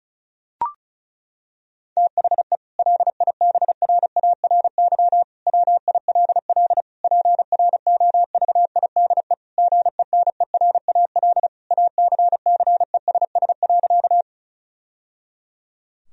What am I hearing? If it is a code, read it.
Morse code, heard as a keyed tone, about 34 wpm, 700 Hz. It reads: THE LIBRARY WILL PROVIDE GENERAL ACCESS.
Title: Sentences from Top 1000 Words - Review Part 4 - 35wpm